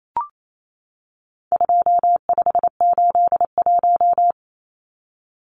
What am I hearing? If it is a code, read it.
Morse code, heard as a keyed tone, about 28 wpm, 700 Hz. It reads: 2581
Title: Numbers - 4-Digit 28wpm